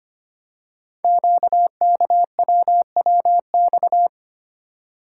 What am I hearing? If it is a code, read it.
Morse code, heard as a keyed tone, about 25 wpm, 700 Hz. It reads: QKWWX
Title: Code Groups - 5 Letters - 25wpm